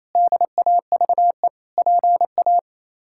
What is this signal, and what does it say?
Morse code, heard as a keyed tone, about 28 wpm, 700 Hz. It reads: DAVE PA
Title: K1USN SST QSO Practice - 28wpm